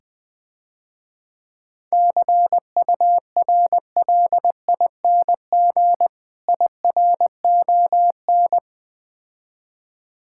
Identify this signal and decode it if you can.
Morse code, heard as a keyed tone, about 20 wpm, 700 Hz. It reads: CURLING IRON
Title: Difficult Words - Speed-Racing with Hints - 20wpm